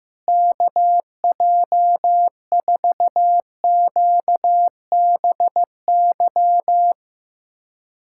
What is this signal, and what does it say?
Morse code, heard as a keyed tone, about 15 wpm, 700 Hz. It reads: KJ4QBY
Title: USA Callsign DE Callsign - 6 Characters - 15wpm